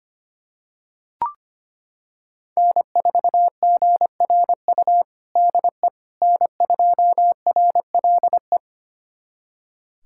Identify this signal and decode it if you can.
Morse code, heard as a keyed tone, about 25 wpm, 700 Hz. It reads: N4GRU DE N2RLE
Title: USA Callsign DE Callsign - 5 Characters - 25wpm